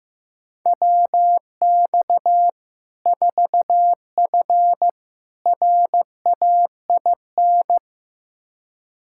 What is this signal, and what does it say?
Morse code, heard as a keyed tone, about 16 wpm, 700 Hz. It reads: WX 4F RAIN
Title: Practice QSO - Exchange 2 - Info - 15wpm